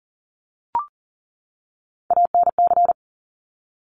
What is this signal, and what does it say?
Morse code, heard as a keyed tone, about 40 wpm, 700 Hz. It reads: ANC
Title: Top Three Letter Combinations in Words 40wpm